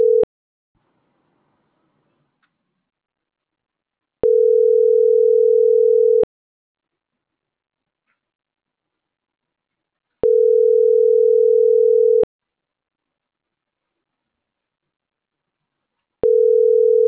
Ringback tone (North America), 440 + 480 Hz.
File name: CENTRAL MAERSK / out gabrielle.melendez